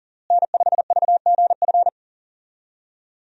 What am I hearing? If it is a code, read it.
Morse code, heard as a keyed tone, about 40 wpm, 700 Hz. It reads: N5VGF
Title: USA Call Signs - 5 Characters - Repeated 3x - 40wpm